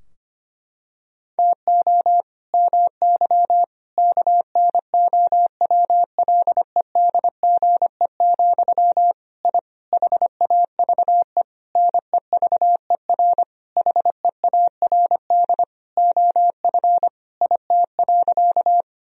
Morse code, heard as a keyed tone, about 25 wpm, 700 Hz. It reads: TO MY KNOWLEDGE, I HAVE NEVER HEARD OF IT.
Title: Sentences from Top 1000 Words - Review Part 3 - 25wpm